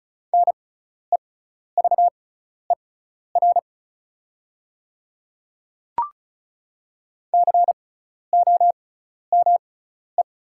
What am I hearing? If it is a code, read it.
Morse code, heard as a keyed tone, about 34 wpm, spaced as for 12 wpm, 700 Hz. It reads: NEVER COME
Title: 6x Character Spacing - Top 300 Words - 35wpm